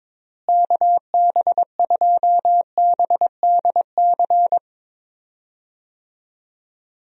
Morse code, heard as a keyed tone, about 22 wpm, 700 Hz. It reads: KB2BDC
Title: USA Call Signs - 6 Characters - Repeated 2x - 22wpm